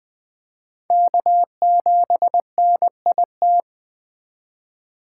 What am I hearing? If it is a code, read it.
Morse code, heard as a keyed tone, about 20 wpm, 700 Hz. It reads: K7NIT
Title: Speed Racing - USA 1x3 Call Signs 17wpm